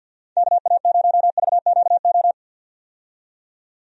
Morse code, heard as a keyed tone, about 51 wpm, 700 Hz. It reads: KA0VQO
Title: USA Call Signs - 6 Characters - 50wpm